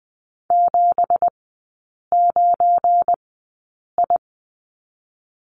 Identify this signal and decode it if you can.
Morse code, heard as a keyed tone, about 20 wpm, spaced as for 8 wpm, 690 Hz. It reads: 79I
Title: 172 Lesson 28e - Number 9 - Three Characters 20wpm